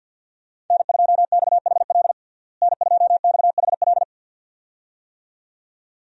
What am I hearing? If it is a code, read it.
Morse code, heard as a keyed tone, about 52 wpm, 700 Hz. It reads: N2XHL N2XHL
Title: USA Call Signs - 5 Characters - Repeated 2x - 50wpm